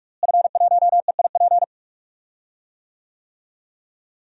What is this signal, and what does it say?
Morse code, heard as a keyed tone, about 44 wpm, 700 Hz. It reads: F1EIP
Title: DXCC-100 Call Signs - 5 Characters - Repeated 3x - 45wpm